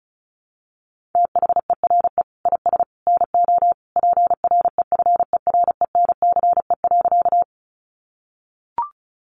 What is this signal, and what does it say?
Morse code, heard as a keyed tone, about 34 wpm, 700 Hz. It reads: THERE IS NO PREFERENCE.
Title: Sentences from Top 1500 Words 35wpm